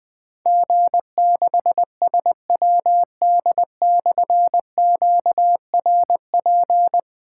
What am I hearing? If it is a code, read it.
Morse code, heard as a keyed tone, about 20 wpm, 700 Hz. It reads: G6SWD/QRP
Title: Difficult Call Signs 20wpm